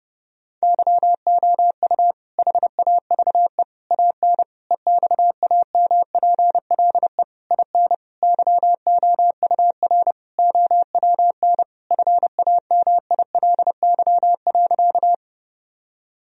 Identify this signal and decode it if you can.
Morse code, heard as a keyed tone, about 30 wpm, 700 Hz. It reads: YOU HAVE AN EXAMPLE IN YOUR OWN FAMILY.
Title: Sentences from Top 1000 Words - Review Part 1 - 30wpm